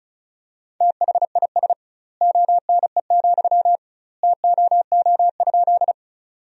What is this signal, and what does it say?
Morse code, heard as a keyed tone, about 36 wpm, 700 Hz. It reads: THIS ONE, TOO?
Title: 2x Word-Spacing - Sentences from Top 200 Words - 35wpm